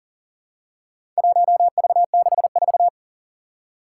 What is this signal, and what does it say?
Morse code, heard as a keyed tone, about 40 wpm, 700 Hz. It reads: 1V64
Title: Four Challenging Letter-Number Combinations 40wpm